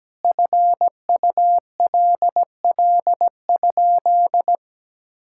Morse code, heard as a keyed tone, about 17 wpm, 700 Hz. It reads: FULL?
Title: Sentences from Top 1600 Words 17wpm